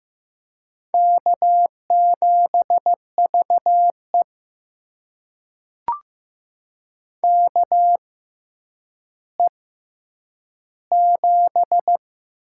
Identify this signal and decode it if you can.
Morse code, heard as a keyed tone, about 15 wpm, spaced as for 9 wpm, 700 Hz. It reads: K7VE KE7
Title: USA Call Signs - 4 Characters - 6x Character Spacing - 15wpm